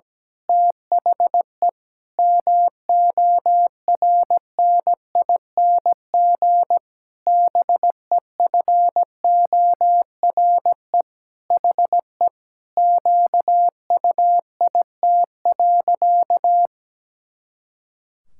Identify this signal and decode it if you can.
Morse code, heard as a keyed tone, about 17 wpm, 700 Hz. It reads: THE MORNING BEFORE HE QUIT.
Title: Pangram Challenge v2 - 17wpm